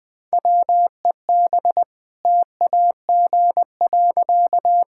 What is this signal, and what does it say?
Morse code, heard as a keyed tone, about 20 wpm, 700 Hz. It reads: WEB TAG.
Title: All-Age Appropriate Jokes - 20wpm